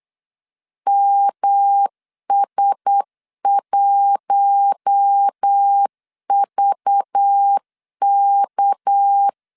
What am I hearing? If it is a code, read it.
Morse code, heard as a keyed tone, about 8 wpm, 800 Hz. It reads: MS1VK
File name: LEZIONI / 6 PKX0 / 6 PK 8wpm 3003